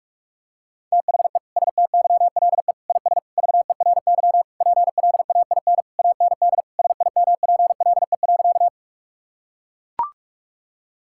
Morse code, heard as a keyed tone, about 41 wpm, 700 Hz. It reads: THE STYLE IS VERY PLAIN AND SIMPLE.
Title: Sentences from Top 1000 Words - Review Part 5 - 45wpm